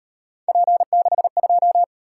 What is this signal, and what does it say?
Morse code, heard as a keyed tone, about 38 wpm, 700 Hz. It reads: P62